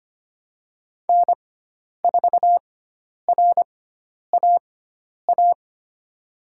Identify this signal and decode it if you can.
Morse code, heard as a keyed tone, about 25 wpm, spaced as for 10 wpm, 700 Hz. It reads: N4RAA
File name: Lesson 16g - Letter c - Call Signs 25wpm